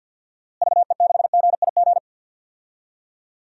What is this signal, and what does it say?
Morse code, heard as a keyed tone, about 51 wpm, 700 Hz. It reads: VE6GIG
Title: DXCC-100 Call Signs - 6 Characters - Repeated 3x - 50wpm